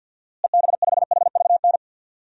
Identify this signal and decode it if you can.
Morse code, heard as a keyed tone, about 46 wpm, 700 Hz. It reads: EB5HVN
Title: DXCC-100 Call Signs - 6 Characters - Repeated 3x - 50wpm